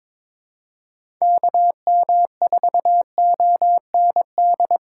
Morse code, heard as a keyed tone, about 21 wpm, 700 Hz. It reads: KM4OND
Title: USA Call Signs - 6 Characters - 2x Character Spacing - 22wpm